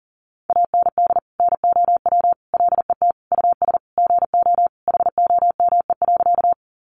Morse code, heard as a keyed tone, about 40 wpm, 700 Hz. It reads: AND NOW LET US GO HOME.
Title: Sentences from Top 300 Words 40wpm